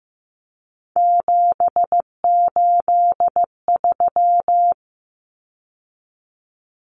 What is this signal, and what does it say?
Morse code, heard as a keyed tone, about 15 wpm, 700 Hz. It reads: 783
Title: Common Cut Numbers - 3-Digit 15wpm